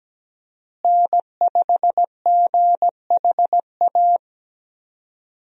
Morse code, heard as a keyed tone, about 17 wpm, 700 Hz. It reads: N5GHA
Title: USA Call Signs - 5 Characters - Repeated 2x - 17wpm